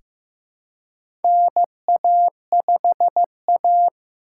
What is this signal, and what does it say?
Morse code, heard as a keyed tone, about 15 wpm, 700 Hz. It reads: NA5A
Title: USA Call Signs - 4x Character Spacing - 15wpm